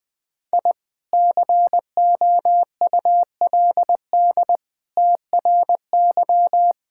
Morse code, heard as a keyed tone, about 20 wpm, 700 Hz. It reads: I COULD TRY
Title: Sentences with 2-Word Phrases - 20wpm